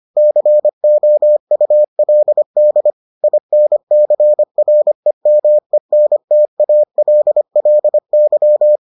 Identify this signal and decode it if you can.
Morse code, heard as a keyed tone, about 25 wpm, 600 Hz. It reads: COULD INCREMENTALLY